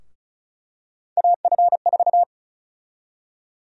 Morse code, heard as a keyed tone, about 36 wpm, 700 Hz. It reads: AF4
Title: USA Call Sign Prefixes - Repeated 2x - 35wpm